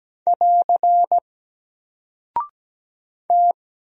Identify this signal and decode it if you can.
Morse code, heard as a keyed tone, about 17 wpm, 700 Hz. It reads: <AR> T
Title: Single Character (Letter:Number:Punctuation:Prosign) 17wpm